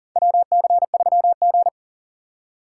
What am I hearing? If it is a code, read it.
Morse code, heard as a keyed tone, about 37 wpm, 700 Hz. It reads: WC3G
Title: USA Call Signs - 4 Characters - 6x Character Spacing - 40wpm